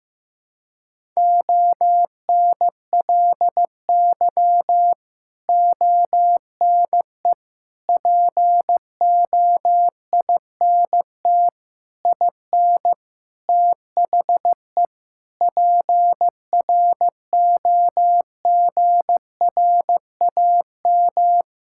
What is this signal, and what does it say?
Morse code, heard as a keyed tone, about 15 wpm, 700 Hz. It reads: ONLY ONE POINT IN THE PROGRAM
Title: Sentences from Top 1000 Words - Review Part 5 - 15wpm